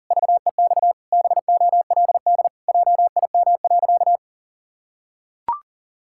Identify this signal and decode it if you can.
Morse code, heard as a keyed tone, about 37 wpm, 700 Hz. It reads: VEX BOLD JIM.
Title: Pangram Challenge v2 - 40wpm